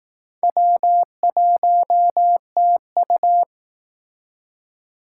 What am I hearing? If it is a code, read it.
Morse code, heard as a keyed tone, about 18 wpm, 700 Hz. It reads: W1TU
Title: Speed Racing - USA 1x2 Call Signs 15wpm